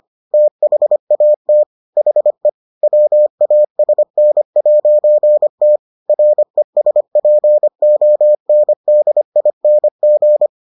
Morse code, heard as a keyed tone, about 25 wpm, 600 Hz. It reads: THAT HE WASN'T RESPONDING